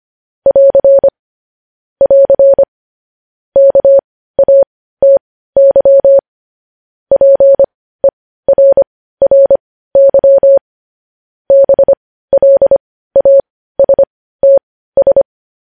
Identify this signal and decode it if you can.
Morse code, heard as a keyed tone, about 25 wpm, spaced as for 15 wpm, 550 Hz. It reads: <AR> <AR> KATY PERRY BLASTS
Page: News Headlines in Morse Code at 15 WPM